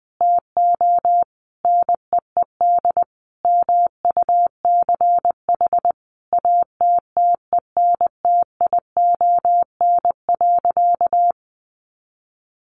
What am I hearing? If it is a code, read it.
Morse code, heard as a keyed tone, about 20 wpm, 700 Hz. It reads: TO NEED MUCH ATTENTION.
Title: Sentences from Top 1000 Words 20wpm